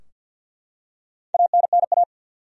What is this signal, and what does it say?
Morse code, heard as a keyed tone, about 53 wpm, 700 Hz. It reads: ANNA
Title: USA Common Names - Max 5 Letters - 50wpm